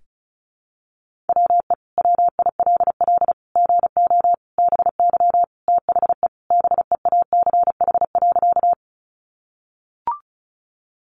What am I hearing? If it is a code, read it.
Morse code, heard as a keyed tone, about 34 wpm, 700 Hz. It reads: WE WILL GO BY THE BEACH.
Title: Sentences from Top 1500 Words 35wpm